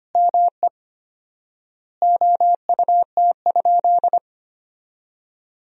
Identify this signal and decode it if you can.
Morse code, heard as a keyed tone, about 25 wpm, 700 Hz. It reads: ME OUT?
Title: 4x Word-Spacing - Sentences from Top 200 Words - 25wpm